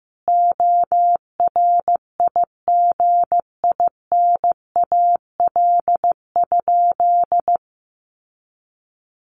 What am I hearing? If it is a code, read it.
Morse code, heard as a keyed tone, about 15 wpm, 700 Hz. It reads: ORIGINAL?
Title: Sentences from Top 900 Words 15wpm